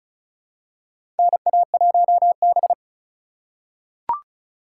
Morse code, heard as a keyed tone, about 36 wpm, 700 Hz. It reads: NA1B
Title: USA Call Signs - 4 Characters - 3x Character Spacing - 35wpm